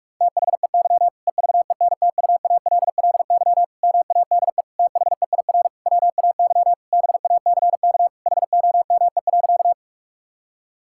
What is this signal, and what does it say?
Morse code, heard as a keyed tone, about 46 wpm, 700 Hz. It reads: THEY EVENTUALLY MADE THEIR WAY BACK HOME.
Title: Sentences from Top 1000 Words - Review Part 6 - 45wpm